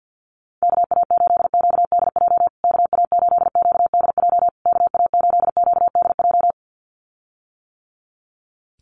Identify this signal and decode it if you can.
Morse code, heard as a keyed tone, about 48 wpm, 700 Hz. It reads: KA8QDJ KA8QDJ KA8QDJ
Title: USA Call Signs Repeated 3x 50wpm